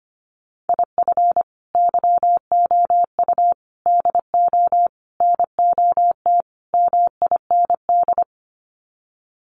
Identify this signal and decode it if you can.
Morse code, heard as a keyed tone, about 25 wpm, 700 Hz. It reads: IF YOU DO NOT MIND
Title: Sets of 5 Words 25wpm